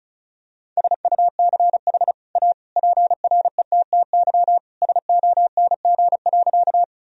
Morse code, heard as a keyed tone, about 34 wpm, 700 Hz. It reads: SUCH A PRETTY SONG.
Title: Sentences from Top 1000 Words - Review Part 2 - 35wpm